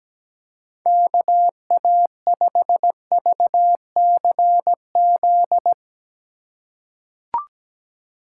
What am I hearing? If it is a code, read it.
Morse code, heard as a keyed tone, about 17 wpm, 700 Hz. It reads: KA5VCZ
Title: USA Call Signs - 6 Characters - 2x Character Spacing - 17wpm